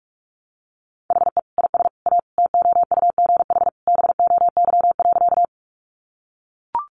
Morse code, heard as a keyed tone, about 44 wpm, 700 Hz. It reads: HE IS A TOUGH BOY.
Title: Sentences from Top 1200 Words 45wpm